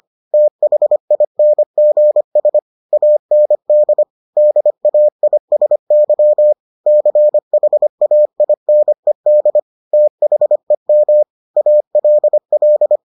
Morse code, heard as a keyed tone, about 25 wpm, 600 Hz. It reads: THINGS AND DAISY CHAINED THEM ALL